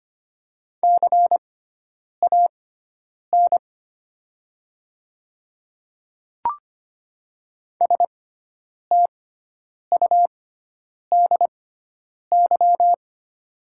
Morse code, heard as a keyed tone, about 25 wpm, spaced as for 9 wpm, 700 Hz. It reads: CAN STUDY
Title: 6x Character Spacing - Top 300 Words - 25wpm